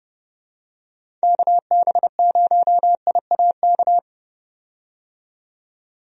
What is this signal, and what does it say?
Morse code, heard as a keyed tone, about 30 wpm, 700 Hz. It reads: KB0IAK
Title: USA Call Signs - 6 Characters - 30wpm